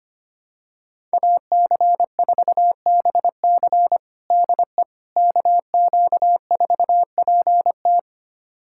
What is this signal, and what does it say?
Morse code, heard as a keyed tone, about 25 wpm, 700 Hz. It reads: AC4BC DE KQ4PT
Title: USA Callsign DE Callsign - 5 Characters - 25wpm